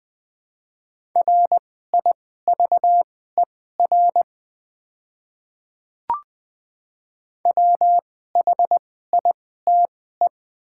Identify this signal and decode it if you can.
Morse code, heard as a keyed tone, about 20 wpm, spaced as for 14 wpm, 700 Hz. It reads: RIVER WHITE
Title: 2x Character Spacing - Top 300 Words - 20wpm